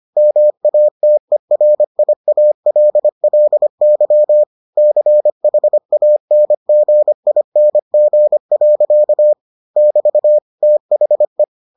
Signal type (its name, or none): Morse code